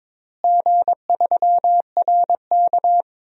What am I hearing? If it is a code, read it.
Morse code, heard as a keyed tone, about 21 wpm, 700 Hz. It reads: G3RK
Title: DXCC-100 Call Signs - 4 Characters - Repeated 2x - 22wpm